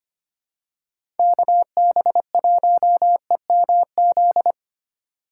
Morse code, heard as a keyed tone, about 25 wpm, 700 Hz. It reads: KB1EMZ
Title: USA Call Signs - 6 Characters - 4x Character Spacing - 25wpm